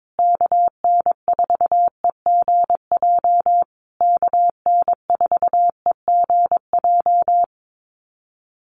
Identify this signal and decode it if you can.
Morse code, heard as a keyed tone, about 22 wpm, 700 Hz. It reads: KN4EGJ KN4EGJ
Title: USA Call Signs Repeated 3x 22wpm